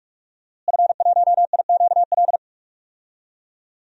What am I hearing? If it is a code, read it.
Morse code, heard as a keyed tone, about 43 wpm, 700 Hz. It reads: F1IQL
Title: DXCC-100 Call Signs - 5 Characters - Repeated 2x - 45wpm